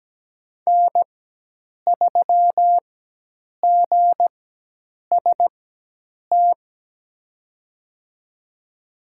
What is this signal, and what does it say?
Morse code, heard as a keyed tone, about 17 wpm, spaced as for 8 wpm, 700 Hz. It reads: N3GST
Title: USA Call Signs - 4x Character Spacing - 17wpm